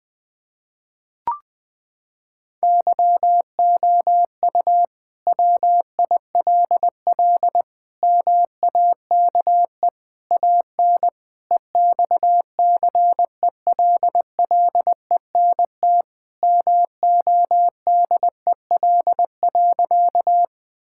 Morse code, heard as a keyed tone, about 21 wpm, 700 Hz. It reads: YOU WILL MAKE AN EXCELLENT MODEL.
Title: Sentences from Top 1000 Words - Review Part 5 - 20wpm